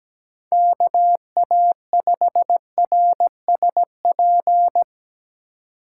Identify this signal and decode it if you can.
Morse code, heard as a keyed tone, about 19 wpm, 700 Hz. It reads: KA5RSP
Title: Speed Racing - USA Call Signs 17wpm